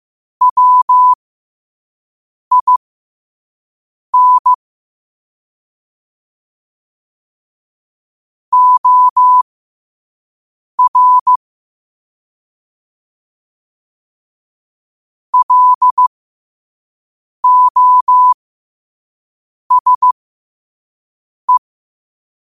Morse code, heard as a keyed tone, about 15 wpm, spaced as for 5 wpm, 1000 Hz. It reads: WIN OR LOSE
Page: New quotes every day in morse code at 5 Words per minute.